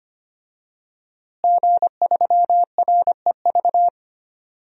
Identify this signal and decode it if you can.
Morse code, heard as a keyed tone, about 25 wpm, 700 Hz. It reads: G3REV
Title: DXCC-100 Call Signs - 5 Characters - Repeated 3x - 25wpm